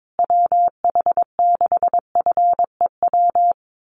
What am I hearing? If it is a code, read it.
Morse code, heard as a keyed tone, about 22 wpm, 700 Hz. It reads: WH6FEW
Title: USA Call Signs Repeated 3x 22wpm